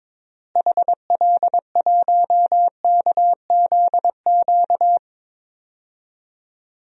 Morse code, heard as a keyed tone, about 22 wpm, 700 Hz. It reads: HL1KZQ
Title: DXCC-100 Call Signs - 6 Characters - Repeated 2x - 22wpm